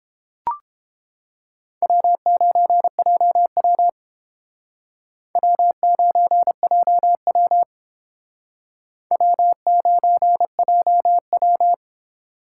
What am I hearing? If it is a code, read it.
Morse code, heard as a keyed tone, about 29 wpm, 700 Hz. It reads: W9JW W9JW W9JW